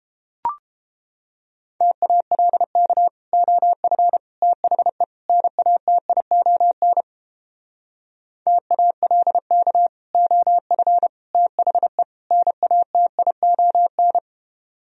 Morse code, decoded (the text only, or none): TALK OF THE NATION TALK OF THE NATION